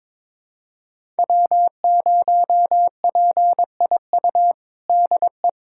Morse code, heard as a keyed tone, about 22 wpm, 700 Hz. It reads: W0PIU DE
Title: USA Callsign DE Callsign - 5 Characters - 22wpm